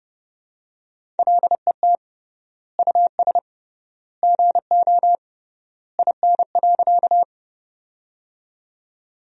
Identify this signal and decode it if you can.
Morse code, heard as a keyed tone, about 31 wpm, 700 Hz. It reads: LET US GO IN.
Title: 3x Word-Spacing - Sentences from Top 300 Words - 30wpm